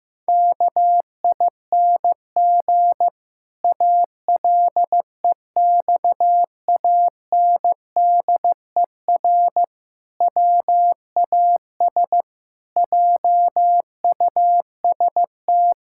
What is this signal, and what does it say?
Morse code, heard as a keyed tone, about 15 wpm, 700 Hz. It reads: KING ALEXANDER WAS JUST
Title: Pangram Challenge v2 - 15wpm